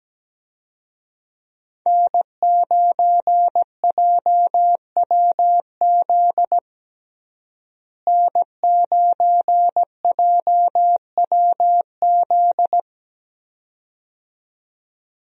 Morse code, heard as a keyed tone, about 17 wpm, 700 Hz. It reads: N9JWZ N9JWZ
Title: USA Call Signs - 5 Characters - Repeated 2x - 17wpm